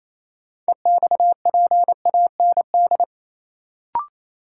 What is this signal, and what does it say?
Morse code, heard as a keyed tone, about 28 wpm, 700 Hz. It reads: EXPAND
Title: Sentences from Top 1400 Words - New Word Drill - 28wpm